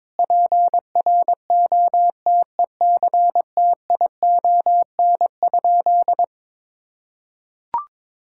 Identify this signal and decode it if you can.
Morse code, heard as a keyed tone, about 22 wpm, 700 Hz. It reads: PROTECTION?
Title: Sentences from Top 1600 Words 22wpm